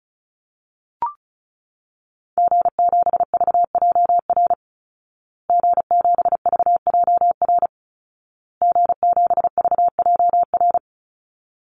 Morse code, heard as a keyed tone, about 34 wpm, 700 Hz. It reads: G7VJR G7VJR G7VJR
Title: Call Signs Repeated 2x - 35wpm